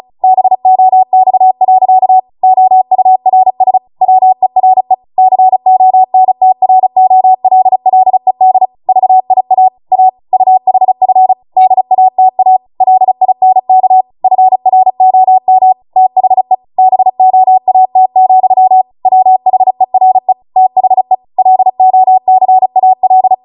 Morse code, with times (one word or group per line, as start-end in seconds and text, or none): 0.2-2.2 BOX.
2.4-3.8 OURS
4.0-4.9 WERE
5.2-8.7 CONTROLLED
8.9-9.7 VIA
9.9-10.1 A
10.3-11.3 UHF
11.6-12.6 DATA
12.8-14.0 LINK
14.2-15.7 FROM
16.0-16.5 THE
16.8-18.8 BOAT,
19.1-20.3 WHERE
20.6-21.1 THE
21.4-23.3 LOCAL